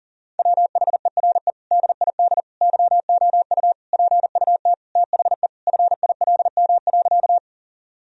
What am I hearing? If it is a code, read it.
Morse code, heard as a keyed tone, about 40 wpm, 700 Hz. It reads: WHERE DID YOU PUT THE FILM.
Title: Sentences from Top 1000 Words - Review Part 3 - 40wpm